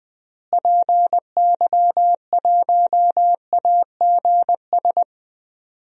Morse code, heard as a keyed tone, about 20 wpm, 700 Hz. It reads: PY1AGS
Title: DXCC-100 Call Signs - 6 Characters - Repeated 2x - 20wpm